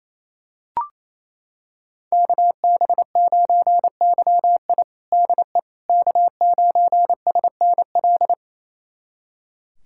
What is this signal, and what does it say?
Morse code, heard as a keyed tone, about 27 wpm, 700 Hz. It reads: KB9YI DE K9SNL
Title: USA Callsign DE Callsign - 5 Characters - 28wpm